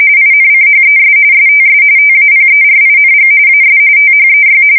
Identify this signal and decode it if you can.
RTTY, about 45.45 baud. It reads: INCHES STS TAPERING AS THEY
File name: fsk170 / USS-New-Jersey